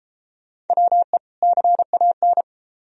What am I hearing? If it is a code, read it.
Morse code, heard as a keyed tone, about 30 wpm, 700 Hz. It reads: WE CAN